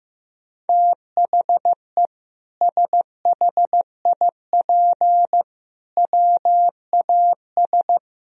Morse code, heard as a keyed tone, about 15 wpm, 700 Hz. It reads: THE SHIP WAS
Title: Sentences from Top 1000 Words - Review Part 3 - 15wpm